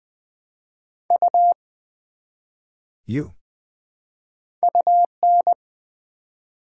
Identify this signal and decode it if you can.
Morse code, heard as a keyed tone, about 20 wpm, 700 Hz. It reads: U UN
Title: Build A Word - 1 Letter At A Time - 5-Letter Words - 20wpm